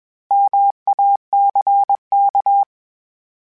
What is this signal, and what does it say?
Morse code, heard as a keyed tone, about 21 wpm, 800 Hz. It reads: MACK
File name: Misc / Misc-mp3s / Misc @20